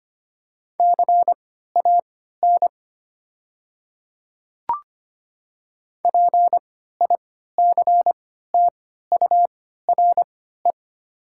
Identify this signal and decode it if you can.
Morse code, heard as a keyed tone, about 25 wpm, spaced as for 14 wpm, 700 Hz. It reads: CAN PICTURE
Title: 3x Character Spacing - Top 300 Words - 25wpm